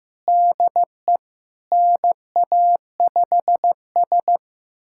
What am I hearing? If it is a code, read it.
Morse code, heard as a keyed tone, about 15 wpm, 700 Hz. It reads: DE NA5S
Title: USA Callsign DE Callsign - 4 Characters - 15wpm